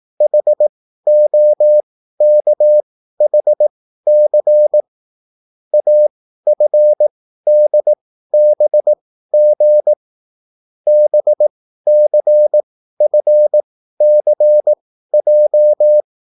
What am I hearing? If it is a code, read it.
Morse code, heard as a keyed tone, about 18 wpm, spaced as for 13 wpm, 600 Hz. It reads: HOKHC AFDBG BCFCJ